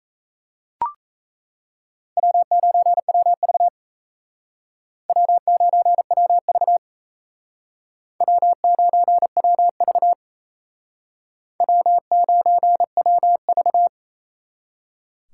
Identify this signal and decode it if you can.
Morse code, heard as a keyed tone, about 38 wpm, 700 Hz. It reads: W9WV W9WV W9WV W9WV